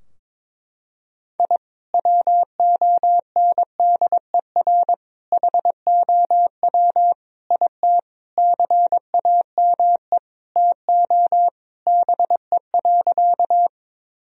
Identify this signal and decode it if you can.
Morse code, heard as a keyed tone, about 22 wpm, 700 Hz. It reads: I WONDER HOW IT CAME TO BE.
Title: Sentences from Top 1000 Words - Review Part 3 - 22wpm